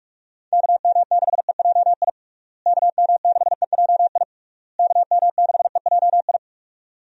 Morse code, heard as a keyed tone, about 43 wpm, 700 Hz. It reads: KM6EJI KM6EJI KM6EJI
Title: USA Call Signs - 6 Characters - Repeated 3x - 45wpm